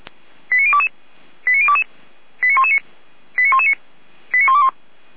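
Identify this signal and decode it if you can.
Five-tone selective call (ZVEI), heard as quick runs of five tones, tones of about 70 ms. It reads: 89022 89022 89119 89119 89121
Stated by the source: ZVEI five-tone call received by SDR.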